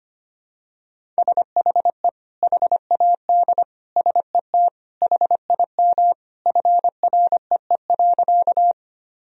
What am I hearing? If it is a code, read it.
Morse code, heard as a keyed tone, about 25 wpm, 700 Hz. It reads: SHE HAD SET HIM FREE.
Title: Sentences from Top 1000 Words - Review Part 1 - 25wpm